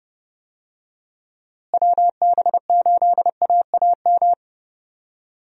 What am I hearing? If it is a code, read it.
Morse code, heard as a keyed tone, about 30 wpm, 700 Hz. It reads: WB8AAM
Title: USA Call Signs - 6 Characters - Repeated 2x - 30wpm